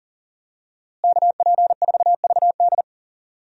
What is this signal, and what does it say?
Morse code, heard as a keyed tone, about 40 wpm, 700 Hz. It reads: KP4VD
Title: DXCC-100 Call Signs - 5 Characters - Repeated 3x - 40wpm